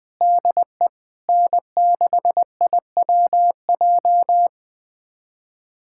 Morse code, heard as a keyed tone, about 20 wpm, 700 Hz. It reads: DE N6IWJ
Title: USA Callsign DE Callsign - 5 Characters - 20wpm